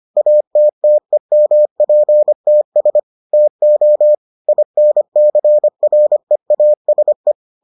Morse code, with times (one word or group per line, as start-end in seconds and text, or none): 0.2-3.0 ATTEMPTS
3.3-4.1 TO
4.5-7.3 INCREASE